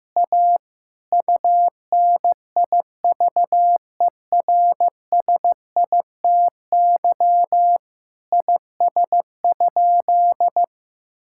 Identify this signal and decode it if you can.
Morse code, heard as a keyed tone, about 15 wpm, 700 Hz. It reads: A UNIVERSITY IS?
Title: Sentences from Top 1000 Words - Review Part 5 - 15wpm